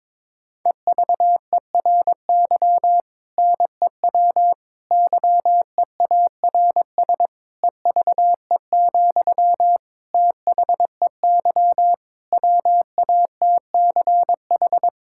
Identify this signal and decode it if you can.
Morse code, heard as a keyed tone, about 22 wpm, 700 Hz. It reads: EVERY NEW YEARS EVE, THEY WATCH
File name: Difficult Sentences with Hints - 22wpm